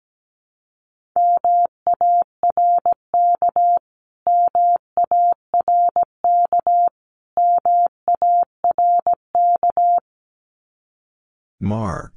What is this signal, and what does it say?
Morse code, heard as a keyed tone, about 17 wpm, 700 Hz. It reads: MARK MARK MARK
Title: Top 300 Words Repeated 17wpm